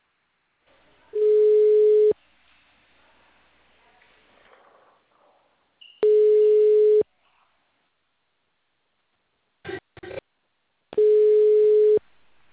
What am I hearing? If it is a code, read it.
Ringback tone (Europe, 425 Hz).
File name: out- -2005